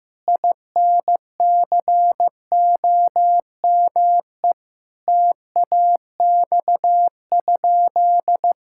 Morse code, heard as a keyed tone, about 15 wpm, 700 Hz. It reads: INCOME TAX?
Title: Sentences from Top 1000 Words - Review Part 5 - 15wpm